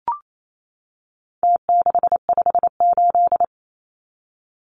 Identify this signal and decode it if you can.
Morse code, heard as a keyed tone, about 28 wpm, 700 Hz. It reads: T658
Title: Numbers - Contest Style - 4-Digit 28wpm